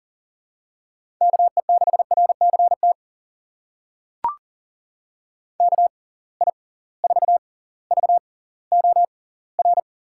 Morse code, heard as a keyed tone, about 37 wpm, spaced as for 20 wpm, 700 Hz. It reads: KE6RCT KI4VOR
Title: USA Call Signs - 6 Characters - 6x Character Spacing - 40wpm